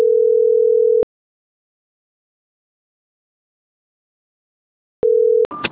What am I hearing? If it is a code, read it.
Ringback tone (North America), 440 + 480 Hz.